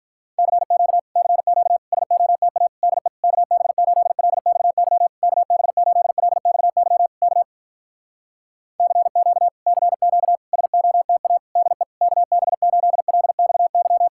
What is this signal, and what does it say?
Morse code, heard as a keyed tone, about 48 wpm, 700 Hz. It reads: CQ CQ SOTA DE KB8LXY KB8LXY K CQ CQ SOTA DE KB8LXY